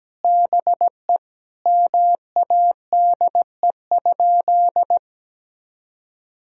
Morse code, heard as a keyed tone, about 17 wpm, 700 Hz. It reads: BE MADE?
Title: Sentences from Top 1000 Words - Review Part 4 - 17wpm